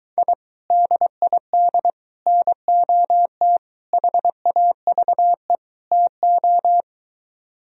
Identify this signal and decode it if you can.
Morse code, heard as a keyed tone, about 24 wpm, 700 Hz. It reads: I DID NOT HAVE TO